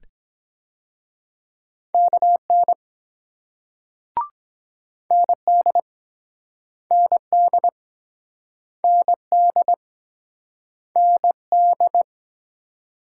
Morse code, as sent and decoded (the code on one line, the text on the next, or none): -.- -. / -. -.. / -. -.. / -. -.. / -. -..
KN ND ND ND ND